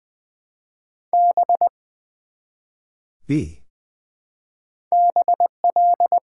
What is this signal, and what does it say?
Morse code, heard as a keyed tone, about 20 wpm, 700 Hz. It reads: B BL
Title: Build A Word - 1 Letter At A Time - 5-Letter Words - 20wpm